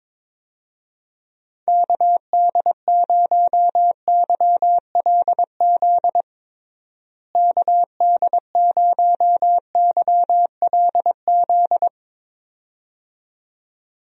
Morse code, heard as a keyed tone, about 22 wpm, 700 Hz. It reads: KD0YLZ KD0YLZ
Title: USA Call Signs - 6 Characters - Repeated 2x - 22wpm